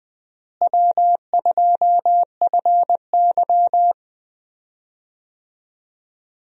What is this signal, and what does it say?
Morse code, heard as a keyed tone, about 20 wpm, 700 Hz. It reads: W2FY
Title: USA Call Signs - 4 Characters - Repeated 3x - 20wpm